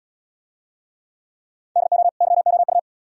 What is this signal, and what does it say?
Morse code, heard as a keyed tone, about 75 wpm, 700 Hz. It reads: DO YOU